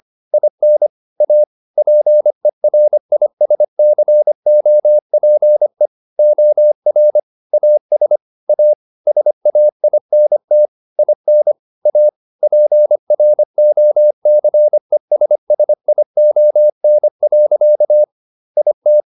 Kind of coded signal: Morse code